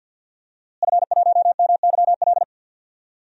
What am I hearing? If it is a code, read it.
Morse code, heard as a keyed tone, about 52 wpm, 700 Hz. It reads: F1MYL
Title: DXCC-100 Call Signs - 5 Characters - Repeated 3x - 50wpm